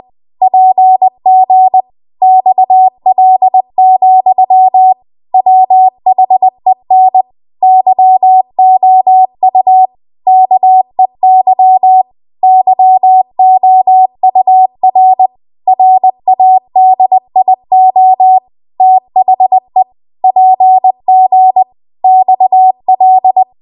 Morse code, heard as a keyed tone, about 20 wpm, 750 Hz. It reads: PG XL, WHEN YOU KEY YOUR RADIO THE PG XL
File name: Archive / 20 WPM / 20WPM